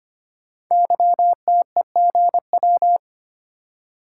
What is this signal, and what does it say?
Morse code, heard as a keyed tone, about 25 wpm, 700 Hz. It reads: YTEGW
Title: Code Groups - 5 Letter:Numbers - 25wpm